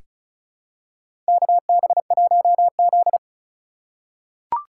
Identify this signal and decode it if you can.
Morse code, heard as a keyed tone, about 34 wpm, 700 Hz. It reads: KB1Z
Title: USA Call Signs - 4 Characters - 4x Character Spacing - 35wpm